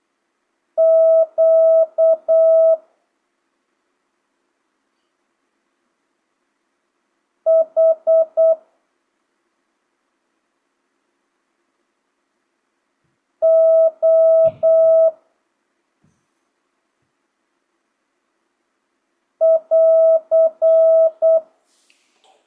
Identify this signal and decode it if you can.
Morse code, heard as a keyed tone, about 8 wpm, 650 Hz. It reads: Q H O <AR>
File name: jh8cbh / 2025cw